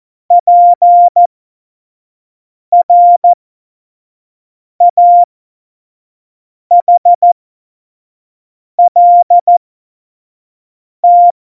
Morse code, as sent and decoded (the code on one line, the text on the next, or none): .--. .-. .- .... .-.. -
PRAHLT